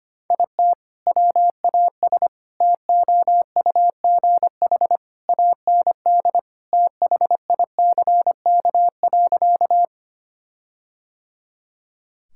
Morse code, heard as a keyed tone, about 25 wpm, 700 Hz. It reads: IT WAS TOUGH AND THICK.
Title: Sentences from Top 1600 Words 25wpm